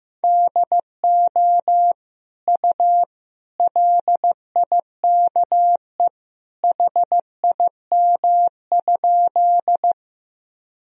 Morse code, heard as a keyed tone, about 16 wpm, 700 Hz. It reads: DO U LIKE HIM?
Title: Sentences from Top 100 Words and Abbreviations 15wpm